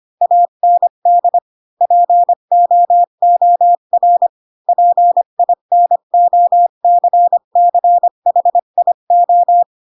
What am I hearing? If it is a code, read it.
Morse code, heard as a keyed tone, about 25 wpm, 700 Hz. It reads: AND POOR PINOCCHIO
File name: Pinocchio / Pinocchio ch7 20 25 ik1hge